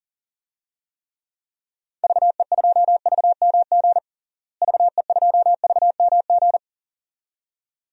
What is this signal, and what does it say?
Morse code, heard as a keyed tone, about 40 wpm, 700 Hz. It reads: VE2VMG VE2VMG
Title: DXCC-100 Call Signs - 6 Characters - Repeated 2x - 40wpm